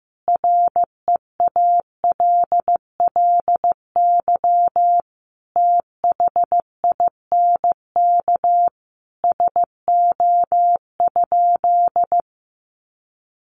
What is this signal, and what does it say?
Morse code, heard as a keyed tone, about 15 wpm, 700 Hz. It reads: REALLY THINK SO?
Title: Sentences from Top 600 Words 15wpm